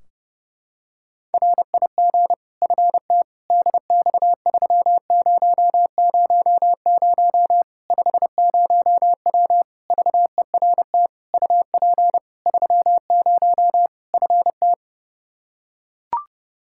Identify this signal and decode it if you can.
Morse code, heard as a keyed tone, about 32 wpm, 700 Hz. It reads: RIG FTDX3000 50W VERT UP 30 FT
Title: Practice QSO - Exchange 2 - Info - 30wpm